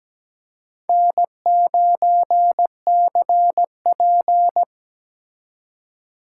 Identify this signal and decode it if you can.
Morse code, heard as a keyed tone, about 20 wpm, 700 Hz. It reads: N9CP
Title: Speed Racing - USA 1x2 Call Signs 17wpm